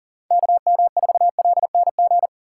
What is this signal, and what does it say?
Morse code, heard as a keyed tone, about 40 wpm, 700 Hz. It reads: KM4LNG
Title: USA Call Signs - 6 Characters - 4x Character Spacing - 40wpm